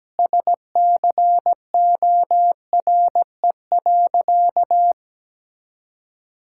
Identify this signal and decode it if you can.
Morse code, heard as a keyed tone, about 17 wpm, 700 Hz. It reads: SCORE.
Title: Sentences from Top 1600 Words 17wpm